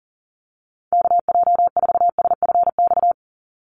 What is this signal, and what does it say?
Morse code, heard as a keyed tone, about 40 wpm, 700 Hz. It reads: KJ4SFX
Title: USA Call Signs 40wpm